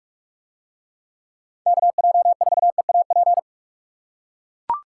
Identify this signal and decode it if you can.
Morse code, heard as a keyed tone, about 44 wpm, 700 Hz. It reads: KJ4EAP